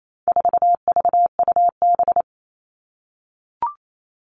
Morse code, heard as a keyed tone, about 28 wpm, 700 Hz. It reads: 4VUB
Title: All Cut Numbers - 4-Digit 28wpm